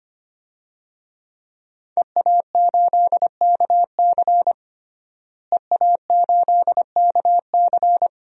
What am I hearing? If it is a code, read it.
Morse code, heard as a keyed tone, about 25 wpm, 700 Hz. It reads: EA8KC EA8KC
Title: DXCC-100 Call Signs - 5 Characters - Repeated 2x - 25wpm